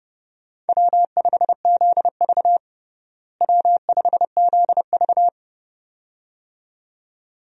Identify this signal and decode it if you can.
Morse code, heard as a keyed tone, about 30 wpm, 700 Hz. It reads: W5ZV W5ZV
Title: USA Call Signs - 4 Characters - Repeated 2x - 30wpm